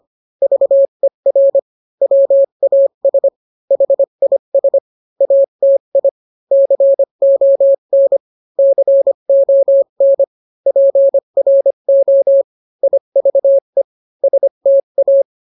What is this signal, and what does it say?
Morse code, heard as a keyed tone, about 25 wpm, 550 Hz. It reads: VER WAS HIS ATI CON CON PRO IVE STA